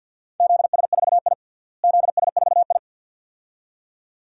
Morse code, heard as a keyed tone, about 52 wpm, 700 Hz. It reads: ZS4I ZS4I
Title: DXCC-100 Call Signs - 4 Characters - Repeated 2x - 50wpm